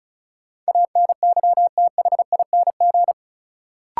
Morse code, heard as a keyed tone, about 36 wpm, 700 Hz. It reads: ANYTHING
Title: Sets of 4 Words - New Word Drill - 35wpm